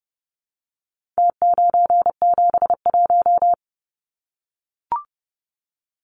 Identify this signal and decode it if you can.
Morse code, heard as a keyed tone, about 30 wpm, 700 Hz. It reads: T971
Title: Numbers - Contest Style - 4-Digit 30wpm